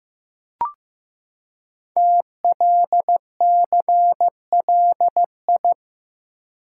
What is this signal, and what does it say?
Morse code, heard as a keyed tone, about 15 wpm, 700 Hz. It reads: TLCLI
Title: Code Groups - 5 Letter:Numbers - 15wpm